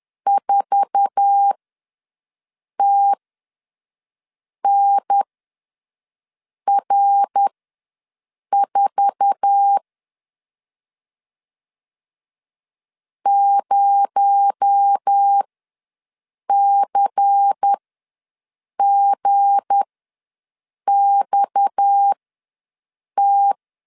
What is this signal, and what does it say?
Morse code, heard as a keyed tone, about 10 wpm, spaced as for 5 wpm, 800 Hz. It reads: 4TNR4 0CGXT
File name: LEZIONI / 8 YJ45 / 8 YJ4 ws 10wpm 4020